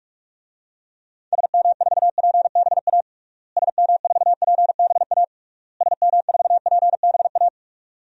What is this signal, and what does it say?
Morse code, heard as a keyed tone, about 41 wpm, 700 Hz. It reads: SM4PBA SM4PBA SM4PBA
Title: DXCC-100 Call Signs - 6 Characters - Repeated 3x - 45wpm